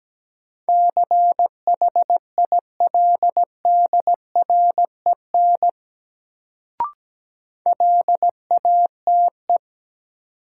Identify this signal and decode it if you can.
Morse code, heard as a keyed tone, about 17 wpm, 700 Hz. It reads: CHILDREN LATE
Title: Sentences from Top 300 Words - New Word Drill - 17wpm